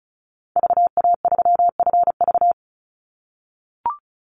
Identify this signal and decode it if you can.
Morse code, heard as a keyed tone, about 34 wpm, 700 Hz. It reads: VA3FV
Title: Call Signs - 35wpm